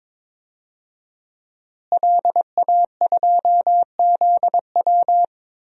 Morse code, heard as a keyed tone, about 22 wpm, 700 Hz. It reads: LA2ZW
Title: DXCC-100 Call Signs - 5 Characters - Repeated 2x - 22wpm